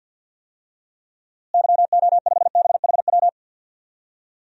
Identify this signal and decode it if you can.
Morse code, heard as a keyed tone, about 46 wpm, 700 Hz. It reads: YO5BHW
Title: DXCC-100 Call Signs - 6 Characters - Repeated 2x - 50wpm